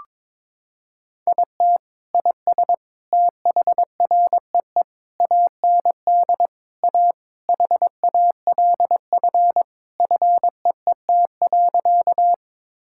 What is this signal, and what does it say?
Morse code, heard as a keyed tone, about 22 wpm, 700 Hz. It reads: IT IS THREE AND A HALF FEET.
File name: practice / 22wpm / Sentences with 3-Word Phrases - 22wpm